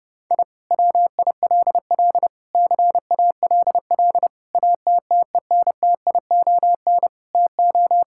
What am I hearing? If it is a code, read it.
Morse code, heard as a keyed tone, about 28 wpm, 700 Hz. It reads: I WILL CALL ATTENTION TO
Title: Sentences from Top 1000 Words - Review Part 6 - 30wpm